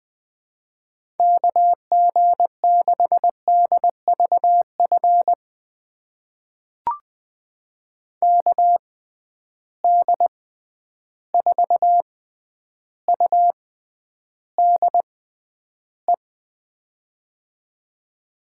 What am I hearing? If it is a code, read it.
Morse code, heard as a keyed tone, about 20 wpm, spaced as for 10 wpm, 700 Hz. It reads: KG6DVF KD4UDE
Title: USA Call Signs - 6 Characters - 6x Character Spacing - 20wpm